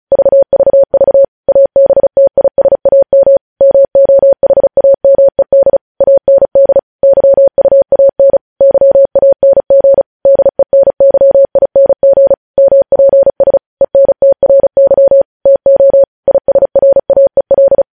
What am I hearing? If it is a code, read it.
Morse code, heard as a keyed tone, about 36 wpm, 550 Hz. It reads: VVV ABTISAM MOHAMED AND YUAN YANG DENYING MPS ENTRY TO ISRAEL
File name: podcast / News headlines-1x-35x35